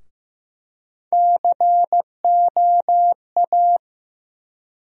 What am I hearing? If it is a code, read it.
Morse code, heard as a keyed tone, about 15 wpm, 700 Hz. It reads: COA
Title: USA Call Sign Postfixes - 3 Letters 15wpm